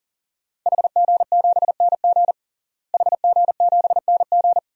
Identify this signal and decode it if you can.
Morse code, heard as a keyed tone, about 38 wpm, 700 Hz. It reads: HG7NG HG7NG
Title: DXCC-100 Call Signs - 5 Characters - Repeated 2x - 40wpm